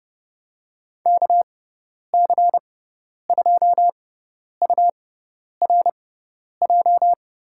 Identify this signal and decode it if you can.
Morse code, heard as a keyed tone, about 32 wpm, spaced as for 10 wpm, 700 Hz. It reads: KC2URJ
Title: USA Call Signs - 6x Character Spacing - 30wpm